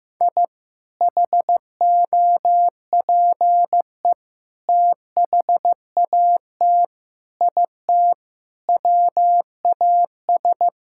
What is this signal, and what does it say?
Morse code, heard as a keyed tone, about 16 wpm, 700 Hz. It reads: I HOPE THAT IT WAS
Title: Sentences from Top 1000 Words - Review Part 5 - 15wpm